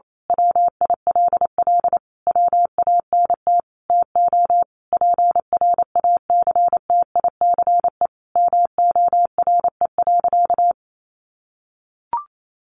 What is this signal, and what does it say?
Morse code, heard as a keyed tone, about 28 wpm, 700 Hz. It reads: WILL WANT TO PRACTICE MORE.
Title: Sentences from Top 700 Words 28wpm